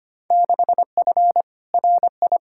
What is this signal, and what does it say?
Morse code, heard as a keyed tone, about 25 wpm, 700 Hz. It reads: 6F RI
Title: Field Day Exchanges Encore - 25wpm